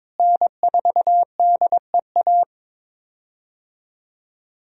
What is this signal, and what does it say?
Morse code, heard as a keyed tone, about 22 wpm, 700 Hz. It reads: N4DEA
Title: USA Call Signs - 5 Characters - Repeated 3x - 22wpm